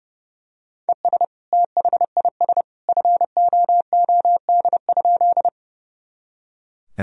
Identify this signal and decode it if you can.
Morse code, heard as a keyed tone, about 30 wpm, 700 Hz. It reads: ES THIS FOOD?
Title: Sentences from Top 300 Words and Abbreviations 30wpm